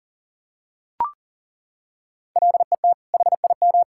Morse code, heard as a keyed tone, about 37 wpm, 700 Hz. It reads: LET HIM
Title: Sentences from Top 1000 Words - Review Part 5 - 40wpm